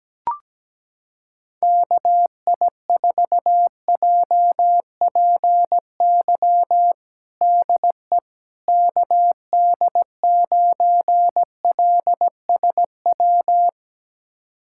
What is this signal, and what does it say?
Morse code, heard as a keyed tone, about 17 wpm, 700 Hz. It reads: KI4JPY DE KD9LSW
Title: USA Callsign DE Callsign 17wpm